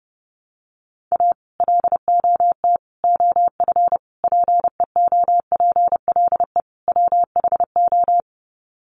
Morse code, heard as a keyed tone, about 30 wpm, 700 Hz. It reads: A LOT OF PEOPLE WHO
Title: Sets of 5 Words 30wpm